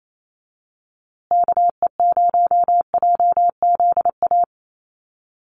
Morse code, heard as a keyed tone, about 28 wpm, 700 Hz. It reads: KE0JZA
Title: USA Call Signs Repeated 2x 28wpm